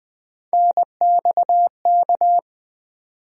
Morse code, heard as a keyed tone, about 20 wpm, 700 Hz. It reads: NXK
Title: USA Call Sign Postfixes - 3 Letters 20wpm